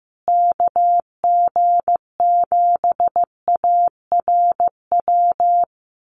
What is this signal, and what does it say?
Morse code, heard as a keyed tone, about 14 wpm, 700 Hz. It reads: KG7ARW
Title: USA Call Signs Repeated 3x 15wpm